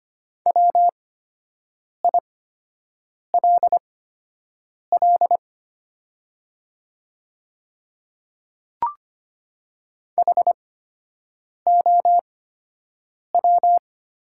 Morse code, heard as a keyed tone, about 25 wpm, spaced as for 7 wpm, 690 Hz. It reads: WILL HOW
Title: 8x Character Spacing - Top 100 Words - 25wpm